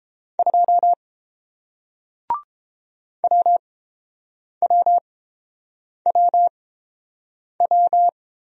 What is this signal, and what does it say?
Morse code, heard as a keyed tone, about 30 wpm, spaced as for 7 wpm, 700 Hz. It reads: 2 WWWW